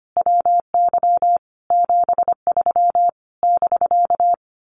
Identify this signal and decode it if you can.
Morse code, heard as a keyed tone, about 25 wpm, 700 Hz. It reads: WY 73 <BK>
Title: Common QSO Phrases Encore 25wpm